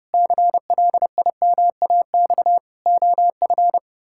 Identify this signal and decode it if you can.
Morse code, heard as a keyed tone, about 30 wpm, 700 Hz. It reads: CLIMAX OF
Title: Pangram Challenge v2 - 30wpm